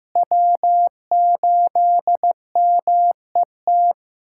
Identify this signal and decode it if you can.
Morse code, heard as a keyed tone, about 15 wpm, 700 Hz. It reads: W8MET
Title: K1USN SST QSO Practice - 15wpm